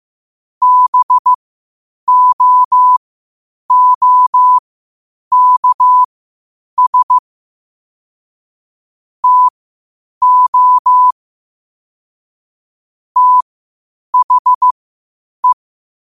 Morse code, heard as a keyed tone, about 15 wpm, spaced as for 8 wpm, 1000 Hz. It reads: BOOKS TO THE
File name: qotdfiles / 8wpm